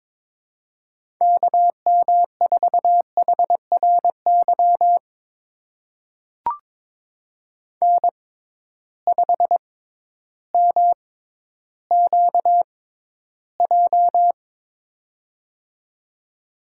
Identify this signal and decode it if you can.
Morse code, heard as a keyed tone, about 22 wpm, spaced as for 12 wpm, 700 Hz. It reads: KM4HRY N5MQJ
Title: USA Call Signs - 6x Character Spacing - 22wpm